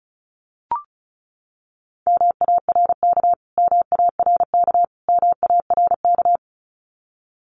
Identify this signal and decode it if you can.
Morse code, heard as a keyed tone, about 34 wpm, 700 Hz. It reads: MARK MARK MARK
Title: Top 300 Words Repeated 35wpm